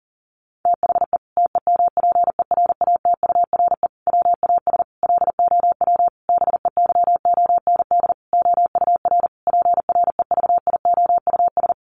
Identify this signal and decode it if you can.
Morse code, heard as a keyed tone, about 40 wpm, 700 Hz. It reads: THE TEMPERATURE WAS LOW BEYOND OUR PREVIOUS
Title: Sentences from Top 1500 Words 40wpm